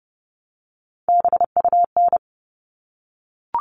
Morse code, heard as a keyed tone, about 30 wpm, 700 Hz. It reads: BUN
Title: All Cut Numbers - 3-Digit 30wpm